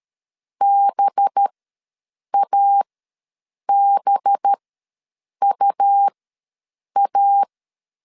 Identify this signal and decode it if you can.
Morse code, heard as a keyed tone, about 13 wpm, spaced as for 7 wpm, 800 Hz. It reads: BABUA
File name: LEZIONI / 5 LBVC / 5 LB ws 12wpm 3034